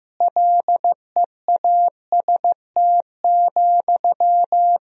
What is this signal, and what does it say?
Morse code, heard as a keyed tone, about 15 wpm, 700 Hz. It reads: LEAST,
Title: Sentences from Top 1000 Words - Review Part 3 - 15wpm